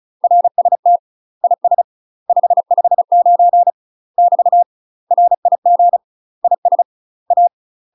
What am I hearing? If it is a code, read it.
Morse code, heard as a keyed tone, about 36 wpm, 700 Hz. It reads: RST IS 559 <BT> RIG IS A
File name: UK MASTER QSOs TRAINING / 35WPM / UK MASTER QSOs 0012